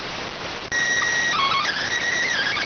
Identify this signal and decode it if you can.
SSTV (slow-scan television), PD 290.